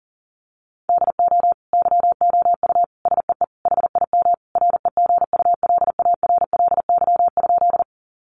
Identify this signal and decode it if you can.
Morse code, heard as a keyed tone, about 40 wpm, 700 Hz. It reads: DO YOU SEE HIM REGULARLY?
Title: Sentences from Top 1400 Words 40wpm